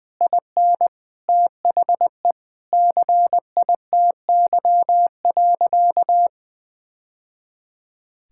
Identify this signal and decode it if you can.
Morse code, heard as a keyed tone, about 20 wpm, 700 Hz. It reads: IN THE CITY.
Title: Sentences with 2-Word Phrases - 20wpm